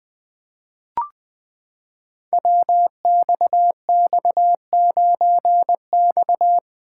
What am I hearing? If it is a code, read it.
Morse code, heard as a keyed tone, about 20 wpm, 700 Hz. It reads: WXX9X
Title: Code Groups - 5 Letter:Numbers - 20wpm